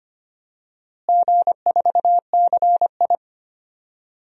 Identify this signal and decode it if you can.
Morse code, heard as a keyed tone, about 25 wpm, 700 Hz. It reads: G4CI
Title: DXCC-100 Call Signs - 4 Characters - Repeated 2x - 25wpm